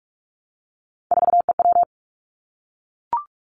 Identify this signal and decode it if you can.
Morse code, heard as a keyed tone, about 46 wpm, 700 Hz. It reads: 4EW
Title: Three Challenging Letter-Number Combinations 45wpm